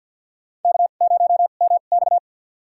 Morse code, heard as a keyed tone, about 52 wpm, spaced as for 36 wpm, 700 Hz. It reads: K0MX
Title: USA Call Signs - 2x Character Spacing - 50wpm